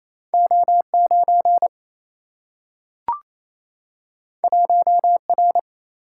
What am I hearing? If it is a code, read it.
Morse code, heard as a keyed tone, about 28 wpm, 700 Hz. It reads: O 9 1 R
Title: Single-Letter Single-Number 28wpm